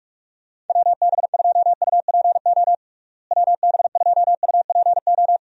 Morse code, heard as a keyed tone, about 42 wpm, 700 Hz. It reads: WB2UPO WB2UPO
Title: USA Call Signs - 6 Characters - Repeated 3x - 45wpm